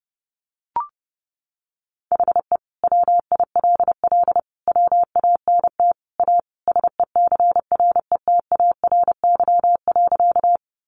Morse code, heard as a keyed tone, about 30 wpm, 700 Hz. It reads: HE WILL WANT A SECRETARY.
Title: Sentences from Top 1300 Words 30wpm